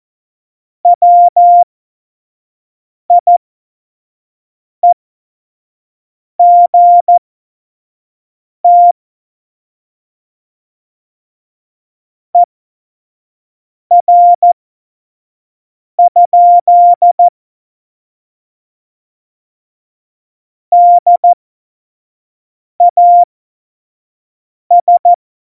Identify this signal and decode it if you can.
Morse code, heard as a keyed tone, about 14 wpm, spaced as for 5 wpm, 700 Hz. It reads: WIEGT ER? DAS
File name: Morsekurs / Witze / Witz 18. Februar 2020, 25 F70